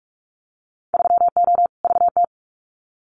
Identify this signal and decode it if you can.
Morse code, heard as a keyed tone, about 44 wpm, 700 Hz. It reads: 3O VT
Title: Winter Field Day Exchanges Encore 45wpm